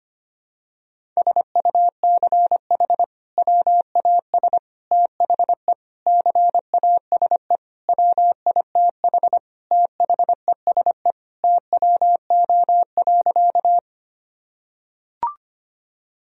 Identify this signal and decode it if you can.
Morse code, heard as a keyed tone, about 25 wpm, 700 Hz. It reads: SUCH WAS THE CASE WITH THESE TWO.
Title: Sentences from Top 1000 Words - Review Part 1 - 25wpm